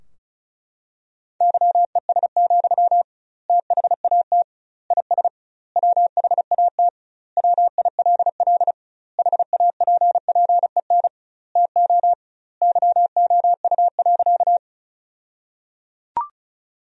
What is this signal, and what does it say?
Morse code, heard as a keyed tone, about 36 wpm, 700 Hz. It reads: YES, THAT IS WHAT WILL HAPPEN TO YOU.
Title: 2x Word-Spacing - Sentences from Top 500 Words - 35wpm